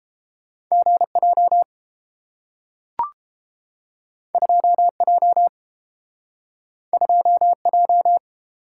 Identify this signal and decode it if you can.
Morse code, heard as a keyed tone, about 31 wpm, 700 Hz. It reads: GJ 2J 2J